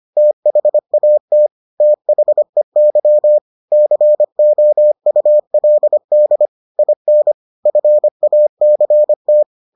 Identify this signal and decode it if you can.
Morse code, heard as a keyed tone, about 25 wpm, 600 Hz. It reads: THAT THEY COULD IN FACT